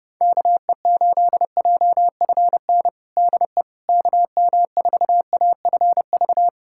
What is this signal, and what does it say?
Morse code, heard as a keyed tone, about 30 wpm, 700 Hz. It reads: KE8JFN DE KM4AFV
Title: USA Callsign DE Callsign - 6 Characters - 30wpm